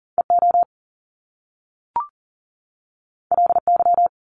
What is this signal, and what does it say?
Morse code, heard as a keyed tone, about 39 wpm, 700 Hz. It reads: EO LY
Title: Top Two Letter Combinations in Words 40wpm